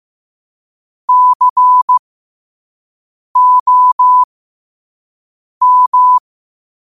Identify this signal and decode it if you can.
Morse code, heard as a keyed tone, about 15 wpm, spaced as for 5 wpm, 1000 Hz. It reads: COM
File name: qotdfiles / 5wpm